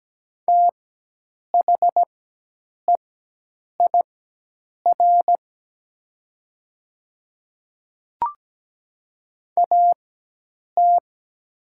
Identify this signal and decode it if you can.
Morse code, heard as a keyed tone, about 17 wpm, spaced as for 8 wpm, 700 Hz. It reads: THEIR AT
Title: 4x Character Spacing - Top 100 Words - 17wpm